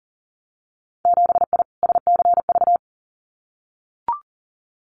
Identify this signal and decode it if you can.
Morse code, heard as a keyed tone, about 40 wpm, 700 Hz. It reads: 7I SCV
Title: Winter Field Day Exchanges Encore 40wpm